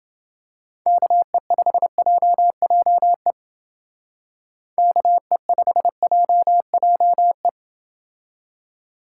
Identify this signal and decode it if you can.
Morse code, heard as a keyed tone, about 29 wpm, 700 Hz. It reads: KE5JJE KE5JJE